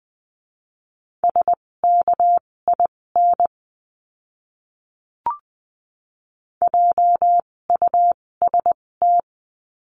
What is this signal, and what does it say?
Morse code, heard as a keyed tone, about 20 wpm, spaced as for 16 wpm, 700 Hz. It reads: SKIN JUST
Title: 256 Lesson 42a - Words 12-20wpm